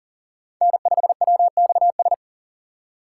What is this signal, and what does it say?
Morse code, heard as a keyed tone, about 40 wpm, 700 Hz. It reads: N5WXS
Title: USA Call Signs - 5 Characters - 4x Character Spacing - 40wpm